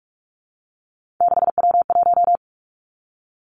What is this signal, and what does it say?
Morse code, heard as a keyed tone, about 46 wpm, 700 Hz. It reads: 6W1
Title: Three Challenging Letter-Number Combinations 45wpm